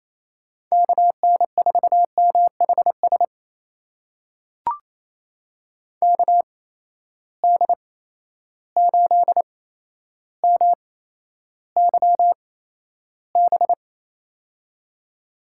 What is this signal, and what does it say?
Morse code, heard as a keyed tone, about 27 wpm, spaced as for 12 wpm, 700 Hz. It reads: KN4MHS KD8MYB
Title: USA Call Signs - 6 Characters - 8x Character Spacing - 28wpm